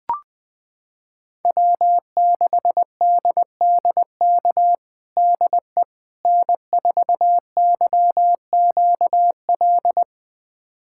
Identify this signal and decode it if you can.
Morse code, heard as a keyed tone, about 20 wpm, 700 Hz. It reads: W6DDK DE N4YQL
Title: USA Callsign DE Callsign - 5 Characters - 20wpm